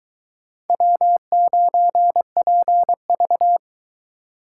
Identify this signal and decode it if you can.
Morse code, heard as a keyed tone, about 22 wpm, 700 Hz. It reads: W9PV